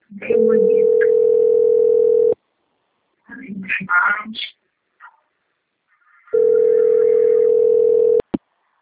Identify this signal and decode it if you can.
Ringback tone (North America), 440 + 480 Hz.